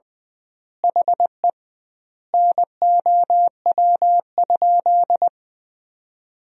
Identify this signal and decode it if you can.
Morse code, heard as a keyed tone, about 20 wpm, 700 Hz. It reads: HE NOW?
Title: 2x Word-Spacing - Sentences from Top 200 Words - 20wpm